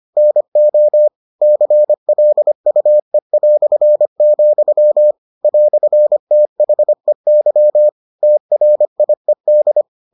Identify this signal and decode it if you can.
Morse code, heard as a keyed tone, about 25 wpm, 600 Hz. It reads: NO CLUE", "THEY TRIED